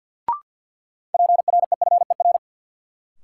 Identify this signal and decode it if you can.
Morse code, heard as a keyed tone, about 48 wpm, 700 Hz. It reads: PREFER
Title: Sentences from Top 1100 Words - New Word Drill - 50wpm